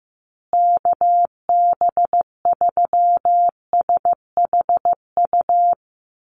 Morse code, heard as a keyed tone, about 15 wpm, 700 Hz. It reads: KB3SHU
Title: USA Call Signs Repeated 3x 15wpm